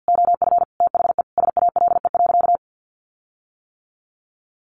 Morse code, heard as a keyed tone, about 50 wpm, 700 Hz. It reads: OF THE SALE.
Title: Sentences from Top 1400 Words 50wpm